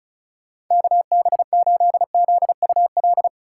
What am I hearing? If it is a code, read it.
Morse code, heard as a keyed tone, about 44 wpm, 700 Hz. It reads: KB8ZUL